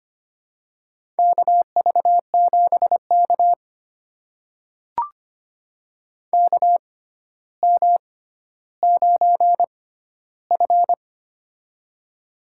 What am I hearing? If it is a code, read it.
Morse code, heard as a keyed tone, about 25 wpm, spaced as for 13 wpm, 700 Hz. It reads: KV7K KM9F
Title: USA Call Signs - 4 Characters - 6x Character Spacing - 25wpm